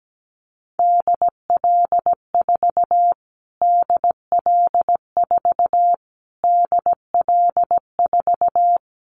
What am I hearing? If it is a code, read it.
Morse code, heard as a keyed tone, about 17 wpm, 700 Hz. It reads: DL4 DL4 DL4
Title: Callsign Prefixes for Top 25 DXCC - Repeated 3x - 17wpm